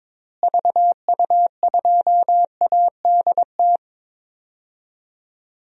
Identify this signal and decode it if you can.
Morse code, heard as a keyed tone, about 22 wpm, 700 Hz. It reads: VU2ADT
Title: DXCC-100 Call Signs - 6 Characters - Repeated 2x - 22wpm